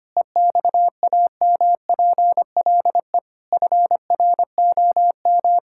Morse code, heard as a keyed tone, about 25 wpm, 700 Hz. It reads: EXAMPLE FROM
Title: Sentences from Top 1000 Words - Review Part 5 - 25wpm